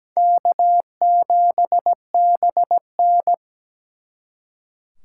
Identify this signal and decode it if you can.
Morse code, heard as a keyed tone, about 20 wpm, 700 Hz. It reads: K7BN
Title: Speed Racing - USA Call Signs 17wpm